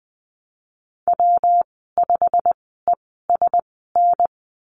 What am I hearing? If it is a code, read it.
Morse code, heard as a keyed tone, about 19 wpm, spaced as for 14 wpm, 700 Hz. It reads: W5ESN
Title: 147 Lesson 24g - Letter y - Call Signs 20wpm